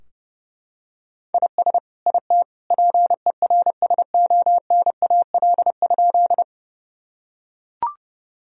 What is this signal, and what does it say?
Morse code, heard as a keyed tone, about 30 wpm, 700 Hz. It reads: IS IT PERSONAL?
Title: Sentences from Top 1000 Words - Review Part 4 - 30wpm